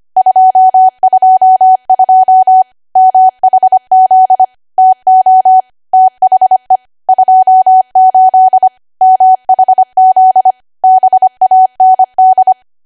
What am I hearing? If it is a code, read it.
Morse code, heard as a keyed tone, about 25 wpm, 750 Hz. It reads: 222 MHZ TO THE 28 MHZ BAND
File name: Archive / 25 WPM / 25WPM